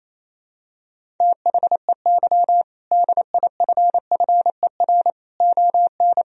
Morse code, heard as a keyed tone, about 28 wpm, 700 Hz. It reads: THEY DIFFER ON
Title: Sentences from Top 1000 Words - Review Part 1 - 28wpm